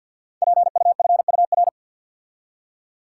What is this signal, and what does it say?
Morse code, heard as a keyed tone, about 51 wpm, 700 Hz. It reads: PUFUR
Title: Code Groups - 5 Letters - 50wpm